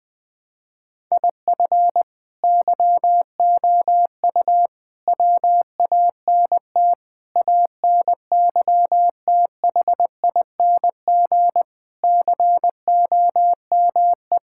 Morse code, heard as a keyed tone, about 20 wpm, 700 Hz. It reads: IF YOU WANT ANYTHING COME
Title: Sentences from Top 1000 Words - Review Part 6 - 20wpm